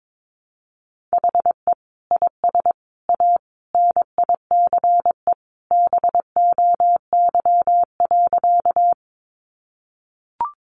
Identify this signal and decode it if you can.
Morse code, heard as a keyed tone, about 22 wpm, 700 Hz. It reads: HE IS A NICE BOY.
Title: Sentences from Top 1100 Words 22wpm